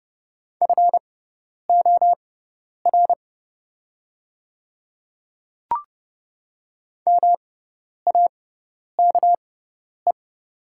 Morse code, heard as a keyed tone, about 30 wpm, spaced as for 10 wpm, 700 Hz. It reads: FOR MAKE
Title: 6x Character Spacing - Top 100 Words - 30wpm